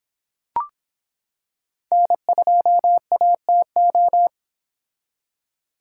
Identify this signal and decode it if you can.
Morse code, heard as a keyed tone, about 25 wpm, 700 Hz. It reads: N2ATO